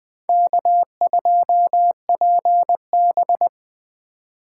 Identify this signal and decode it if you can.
Morse code, heard as a keyed tone, about 20 wpm, 700 Hz. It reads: K2PB